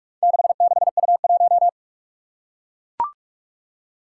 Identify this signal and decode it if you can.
Morse code, heard as a keyed tone, about 44 wpm, 700 Hz. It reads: 66U1
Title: Four Challenging Letter-Number Combinations 45wpm